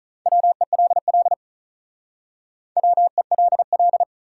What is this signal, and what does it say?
Morse code, heard as a keyed tone, about 46 wpm, 700 Hz. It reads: WELL WELL